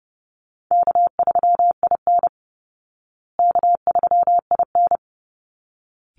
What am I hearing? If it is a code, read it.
Morse code, heard as a keyed tone, about 30 wpm, 700 Hz. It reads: K3IN K3IN
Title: Call Signs Repeated 3x - 30wpm